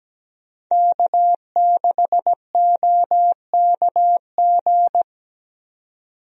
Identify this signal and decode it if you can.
Morse code, heard as a keyed tone, about 17 wpm, 700 Hz. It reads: K6OKG
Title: USA Call Signs - 5 Characters - 17wpm